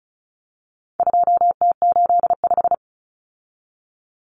Morse code, heard as a keyed tone, about 34 wpm, 700 Hz. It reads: 2T85
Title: Common Cut Numbers - 4-Digit 35wpm